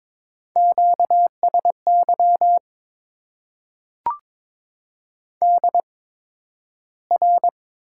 Morse code, heard as a keyed tone, about 22 wpm, spaced as for 12 wpm, 700 Hz. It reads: QSY DR
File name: Top 90 QSO Elements - 8x Character Spacing - 22wpm